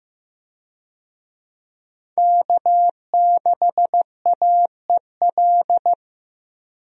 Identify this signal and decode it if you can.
Morse code, heard as a keyed tone, about 15 wpm, 700 Hz. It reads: K6AEL
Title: USA Call Signs - 5 Characters - 15wpm